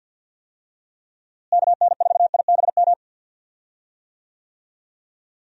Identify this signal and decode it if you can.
Morse code, heard as a keyed tone, about 51 wpm, 700 Hz. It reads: KN4IBM
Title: USA Call Signs - 6 Characters - Repeated 2x - 50wpm